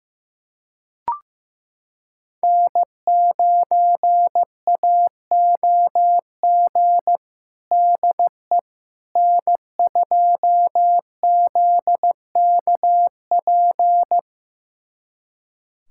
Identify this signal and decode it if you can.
Morse code, heard as a keyed tone, about 15 wpm, 700 Hz. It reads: N9AOG DE N2ZKP
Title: USA Callsign DE Callsign - 5 Characters - 15wpm